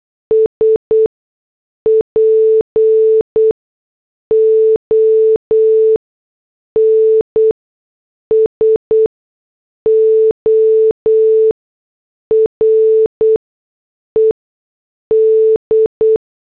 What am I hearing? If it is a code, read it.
Morse code, heard as a keyed tone, about 8 wpm, spaced as for 6 wpm, 440 Hz. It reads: SPONSORED